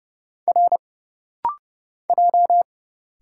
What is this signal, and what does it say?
Morse code, heard as a keyed tone, about 30 wpm, 700 Hz. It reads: R J
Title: Single Letters - Rapid-Fire 30wpm